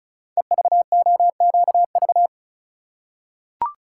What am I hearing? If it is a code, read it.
Morse code, heard as a keyed tone, about 36 wpm, 690 Hz. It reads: EVOQV
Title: Code Groups - 5 Letters - 35wpm